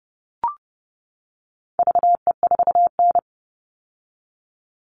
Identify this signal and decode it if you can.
Morse code, heard as a keyed tone, about 30 wpm, 700 Hz. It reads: VE4N
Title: All Cut Numbers - 4-Digit 30wpm